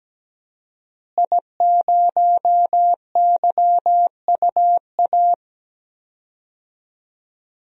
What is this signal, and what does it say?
Morse code, heard as a keyed tone, about 17 wpm, 700 Hz. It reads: I0YUA
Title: DXCC-100 Call Signs - 5 Characters - Repeated 3x - 17wpm